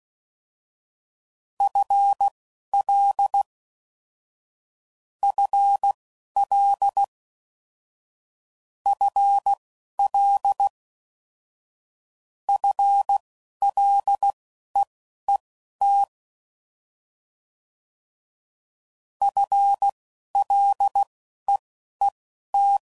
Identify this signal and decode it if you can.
Morse code, heard as a keyed tone, about 16 wpm, spaced as for 11 wpm, 800 Hz. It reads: FL FL FL FLEET FLEET
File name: Basic Sets / QST de WA7RSO / Combnations